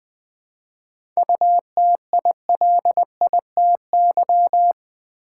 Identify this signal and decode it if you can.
Morse code, heard as a keyed tone, about 20 wpm, 700 Hz. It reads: UTILITY
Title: 7 Letter Words - First 3 Letters Given - 20wpm